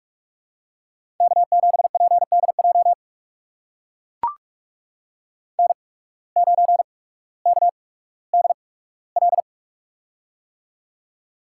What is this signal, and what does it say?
Morse code, heard as a keyed tone, about 44 wpm, spaced as for 19 wpm, 690 Hz. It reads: K7PDJ N9KDL
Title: USA Call Signs - 5 Characters - 8x Character Spacing - 45wpm